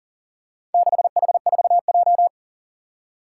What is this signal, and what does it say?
Morse code, heard as a keyed tone, about 40 wpm, 700 Hz. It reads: 6H4J
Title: Four Challenging Letter-Number Combinations 40wpm